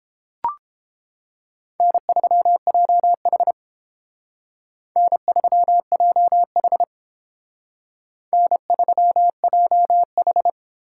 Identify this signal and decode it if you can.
Morse code, heard as a keyed tone, about 32 wpm, 700 Hz. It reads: N3JH N3JH N3JH